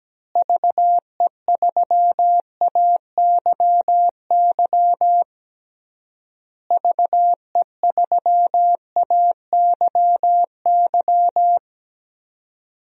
Morse code, heard as a keyed tone, about 17 wpm, 700 Hz. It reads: VE3AYY VE3AYY
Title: DXCC-100 Call Signs - 6 Characters - Repeated 3x - 17wpm